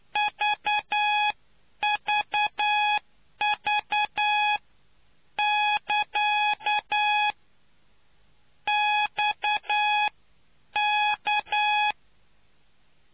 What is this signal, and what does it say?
Morse code, heard as a keyed tone, about 9 wpm, spaced as for 8 wpm, 840 Hz. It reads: VVV <KA> XK